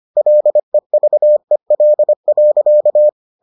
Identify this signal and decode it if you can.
Morse code, heard as a keyed tone, about 25 wpm, 600 Hz. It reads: LEVEL.